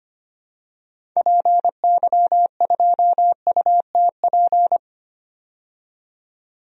Morse code, heard as a keyed tone, about 25 wpm, 700 Hz. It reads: PY2UTP
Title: DXCC-100 Call Signs - 6 Characters - 25wpm